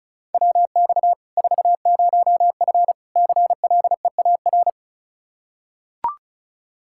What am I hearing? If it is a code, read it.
Morse code, heard as a keyed tone, about 36 wpm, 700 Hz. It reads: WX 40F CLEAR
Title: Practice QSO - Exchange 2 - Info - 35wpm